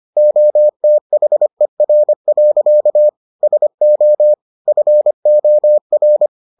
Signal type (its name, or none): Morse code